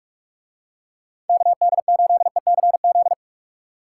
Morse code, heard as a keyed tone, about 45 wpm, 700 Hz. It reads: KD8ECZ